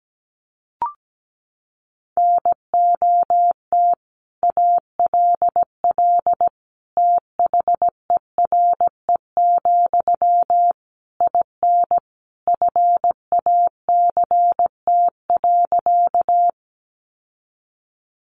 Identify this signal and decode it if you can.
Morse code, heard as a keyed tone, about 17 wpm, 700 Hz. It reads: NOT ALL THERE, IN FACT.
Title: Sentences from Top 600 Words 17wpm